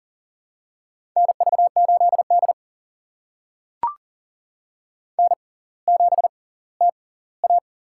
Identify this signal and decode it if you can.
Morse code, heard as a keyed tone, about 40 wpm, spaced as for 20 wpm, 690 Hz. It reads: NV8D N7TA
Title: USA Call Signs - 4 Characters - 6x Character Spacing - 40wpm